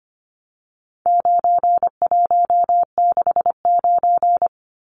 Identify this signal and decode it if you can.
Morse code, heard as a keyed tone, about 25 wpm, 700 Hz. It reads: 9169
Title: Numbers - 4-Digit 25wpm